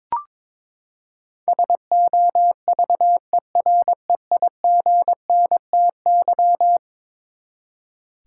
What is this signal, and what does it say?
Morse code, heard as a keyed tone, about 22 wpm, 700 Hz. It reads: SOVEREIGNTY
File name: practice / 22wpm / Eleven Letter Words - 22wpm